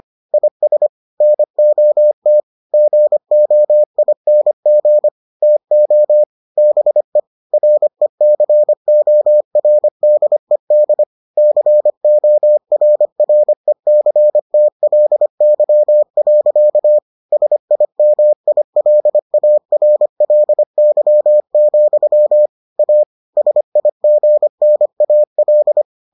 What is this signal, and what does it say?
Morse code, heard as a keyed tone, about 25 wpm, 600 Hz. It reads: IS NOT GOING TO BE RECORDED CORRECTLY. SIMILARLY, A SIGNAL